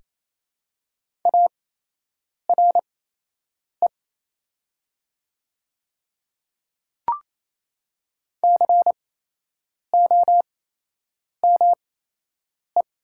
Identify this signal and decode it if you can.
Morse code, heard as a keyed tone, about 27 wpm, spaced as for 8 wpm, 690 Hz. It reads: ARE COME
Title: 8x Character Spacing - Top 200 Words - 28wpm